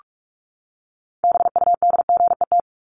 Morse code, heard as a keyed tone, about 44 wpm, 700 Hz. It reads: BUDGET
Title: Top 1001-1250 Words 45wpm